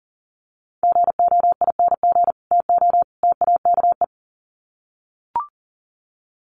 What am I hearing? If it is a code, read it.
Morse code, heard as a keyed tone, about 38 wpm, 700 Hz. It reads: GOING TO TAKE
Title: Sets of 3 Words 40wpm